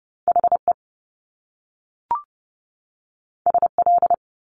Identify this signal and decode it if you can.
Morse code, heard as a keyed tone, about 30 wpm, 700 Hz. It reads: HE SL
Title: Top Two Letter Combinations in Words 30wpm